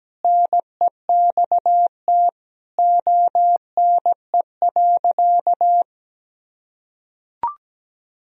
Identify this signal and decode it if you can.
Morse code, heard as a keyed tone, about 17 wpm, 700 Hz. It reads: NEXT ONE.
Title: Sentences from Top 1000 Words - Review Part 1 - 17wpm